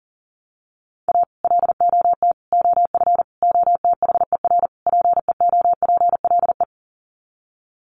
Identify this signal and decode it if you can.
Morse code, heard as a keyed tone, about 40 wpm, 700 Hz. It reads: A LOT OF OTHER PEOPLE
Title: Sets of 5 Words Encore2 40wpm